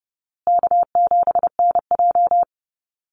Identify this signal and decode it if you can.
Morse code, heard as a keyed tone, about 30 wpm, 700 Hz. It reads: K7NJ
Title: Easy Call Signs Repeated 3x - 30wpm